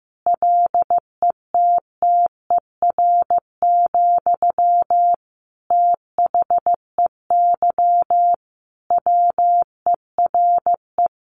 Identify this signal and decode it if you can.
Morse code, heard as a keyed tone, about 15 wpm, 700 Hz. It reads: LETTER, THEY WERE
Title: Sentences from Top 1500 Words 15wpm